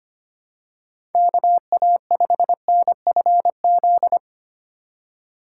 Morse code, heard as a keyed tone, about 29 wpm, 700 Hz. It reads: KA5NFZ
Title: Speed Racing - USA 2x3 Call Signs 25wpm